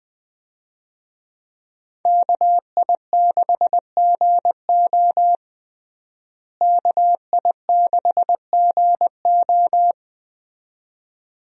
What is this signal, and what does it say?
Morse code, heard as a keyed tone, about 20 wpm, 700 Hz. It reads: KI6GO KI6GO
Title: USA Call Signs - 5 Characters - Repeated 2x - 20wpm